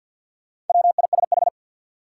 Morse code, heard as a keyed tone, about 47 wpm, 700 Hz. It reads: WISH
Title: Sentences from Top 1600 Words - New Word Drill - 50wpm